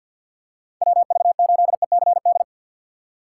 Morse code, heard as a keyed tone, about 53 wpm, 700 Hz. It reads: WV8ECD
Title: USA Call Signs - 6 Characters - 6x Character Spacing - 50wpm